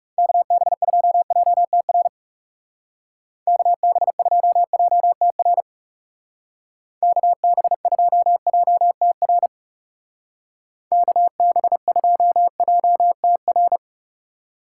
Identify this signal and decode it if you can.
Morse code, heard as a keyed tone, about 37 wpm, 700 Hz. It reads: KB2JTR KB2JTR KB2JTR KB2JTR
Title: Speed Racing - USA Call Signs 30wpm